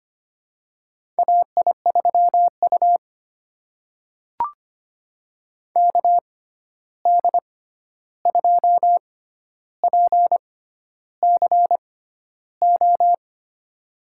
Morse code, heard as a keyed tone, about 25 wpm, spaced as for 11 wpm, 700 Hz. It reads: AI3U KD2PCO
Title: USA Call Signs - 6x Character Spacing - 25wpm